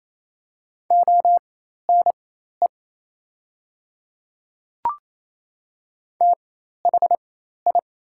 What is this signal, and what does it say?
Morse code, heard as a keyed tone, about 28 wpm, spaced as for 13 wpm, 700 Hz. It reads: ONE THI
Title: 4x Character Spacing - Top 100 Words - 28wpm